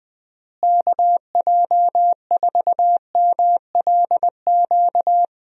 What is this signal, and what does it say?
Morse code, heard as a keyed tone, about 20 wpm, 700 Hz. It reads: KJ4MLQ
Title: USA Call Signs - 6 Characters - Repeated 3x - 20wpm